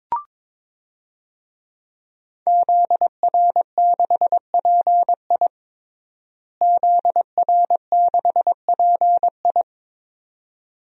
Morse code, heard as a keyed tone, about 21 wpm, 700 Hz. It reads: ZR6PI ZR6PI
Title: DXCC-100 Call Signs - 5 Characters - Repeated 2x - 22wpm